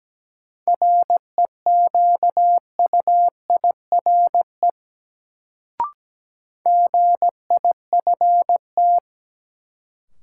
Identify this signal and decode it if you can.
Morse code, heard as a keyed tone, about 17 wpm, 700 Hz. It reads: REQUIRE GIFT
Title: Sentences from Top 900 Words - New Word Drill - 17wpm